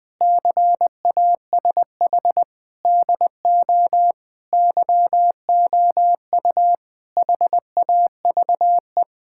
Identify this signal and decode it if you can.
Morse code, heard as a keyed tone, about 21 wpm, 700 Hz. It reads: CASH DO YOU HAVE
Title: Sentences from Top 1000 Words - Review Part 5 - 20wpm